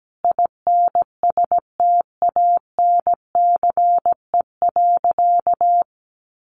Morse code, heard as a keyed tone, about 17 wpm, 700 Hz. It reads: INSTANCE.
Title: Sentences from Top 800 Words 17wpm